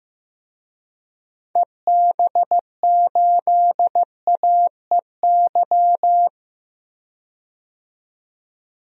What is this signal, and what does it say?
Morse code, heard as a keyed tone, about 16 wpm, 700 Hz. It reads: EB8AEY
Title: DXCC-100 Call Signs - 6 Characters - 15wpm